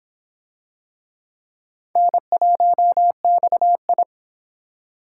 Morse code, heard as a keyed tone, about 26 wpm, 700 Hz. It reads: N1XI